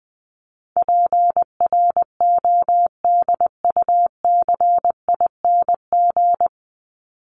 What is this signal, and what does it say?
Morse code, heard as a keyed tone, about 20 wpm, 700 Hz. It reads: PRODUCING
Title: ING Suffix - Top 100 Words with Base Words 20wpm